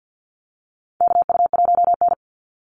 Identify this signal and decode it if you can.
Morse code, heard as a keyed tone, about 50 wpm, 700 Hz. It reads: KU1N
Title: Call Signs Repeated 2x - 50wpm